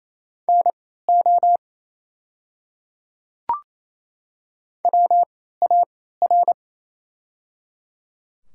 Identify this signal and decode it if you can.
Morse code, heard as a keyed tone, about 28 wpm, spaced as for 16 wpm, 700 Hz. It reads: NO WAR
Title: 3x Character Spacing - Top 500 Words - 28wpm